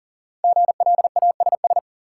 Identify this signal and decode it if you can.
Morse code, heard as a keyed tone, about 37 wpm, 700 Hz. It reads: GLASS
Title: Sentences from Top 1000 Words - Review Part 3 - New Word Drill - 40wpm